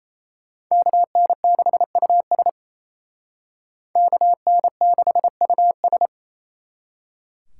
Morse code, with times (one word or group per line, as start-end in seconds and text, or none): 0.7-2.5 KN6US
4.0-6.1 KN6US